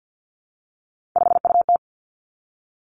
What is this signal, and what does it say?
Morse code, heard as a keyed tone, about 50 wpm, 700 Hz. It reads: 5UT
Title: Three Challenging Letter-Number Combinations 50wpm